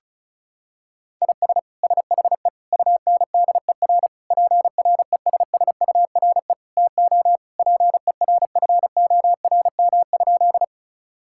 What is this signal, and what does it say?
Morse code, heard as a keyed tone, about 36 wpm, 700 Hz. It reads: IS SHE UNDER PRESSURE TO PERFORM?
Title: Sentences from Top 1000 Words - Review Part 6 - 35wpm